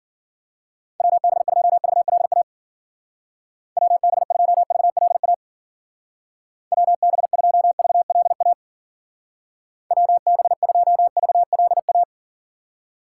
Morse code, heard as a keyed tone, about 51 wpm, 700 Hz. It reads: WB2VLA WB2VLA WB2VLA WB2VLA